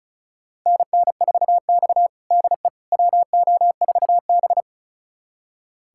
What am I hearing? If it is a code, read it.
Morse code, heard as a keyed tone, about 36 wpm, 700 Hz. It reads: NN4X DE WO4B
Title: USA Callsign DE Callsign - 4 Characters - 35wpm